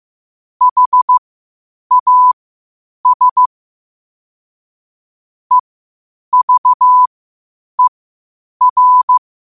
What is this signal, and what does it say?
Morse code, heard as a keyed tone, about 15 wpm, spaced as for 8 wpm, 1000 Hz. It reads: HAS EVER
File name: qotdfiles / 8wpm